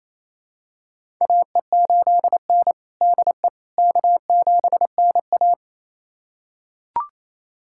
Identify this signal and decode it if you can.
Morse code, heard as a keyed tone, about 29 wpm, 690 Hz. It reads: AE8N DE K7NA
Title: USA Callsign DE Callsign - 4 Characters - 28wpm